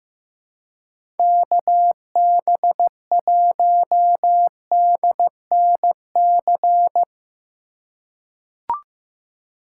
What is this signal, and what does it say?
Morse code, heard as a keyed tone, about 16 wpm, 700 Hz. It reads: KB1DNC
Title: USA Call Signs - 6 Characters - 3x Character Spacing - 15wpm